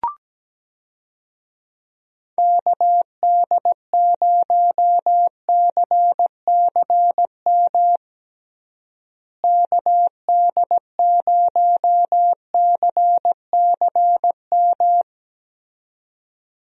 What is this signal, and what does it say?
Morse code, heard as a keyed tone, about 17 wpm, 700 Hz. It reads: KD0CCM KD0CCM
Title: USA Call Signs - 6 Characters - Repeated 2x - 17wpm